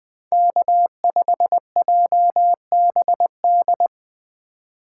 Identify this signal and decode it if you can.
Morse code, heard as a keyed tone, about 24 wpm, 700 Hz. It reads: K5JBD
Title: Speed Racing - USA Call Signs 20wpm